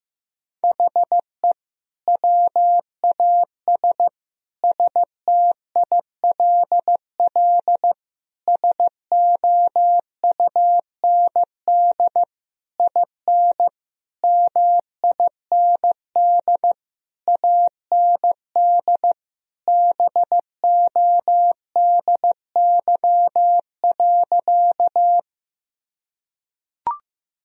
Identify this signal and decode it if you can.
Morse code, heard as a keyed tone, about 15 wpm, 700 Hz. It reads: HE WAS STILL SOUND IN MIND AND BODY.
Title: Sentences from Top 1000 Words - Review Part 2 - 15wpm